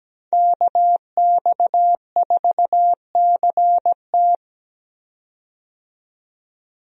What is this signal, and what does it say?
Morse code, heard as a keyed tone, about 17 wpm, 700 Hz. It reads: KX4CT
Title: USA Call Signs - 5 Characters - 17wpm